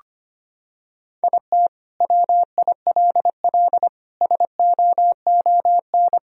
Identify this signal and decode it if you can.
Morse code, heard as a keyed tone, about 25 wpm, 700 Hz. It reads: IT WILL SOON
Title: Sentences from Top 1000 Words - Review Part 2 - 25wpm